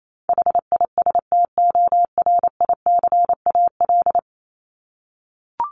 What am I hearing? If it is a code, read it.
Morse code, heard as a keyed tone, about 28 wpm, 700 Hz. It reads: HISTORICAL
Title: Top 501-750 Words 28wpm